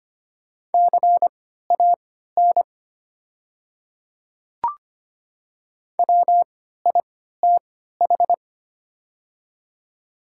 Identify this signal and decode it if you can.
Morse code, heard as a keyed tone, about 25 wpm, spaced as for 14 wpm, 700 Hz. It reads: CAN WITH
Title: 3x Character Spacing - Top 100 Words - 25wpm